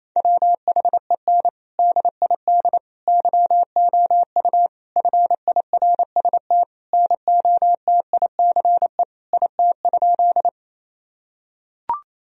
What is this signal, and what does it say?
Morse code, heard as a keyed tone, about 28 wpm, 700 Hz. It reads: WHEN DID YOU FIRST NOTICE IT?
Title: Sentences from Top 1000 Words - Review Part 2 - 28wpm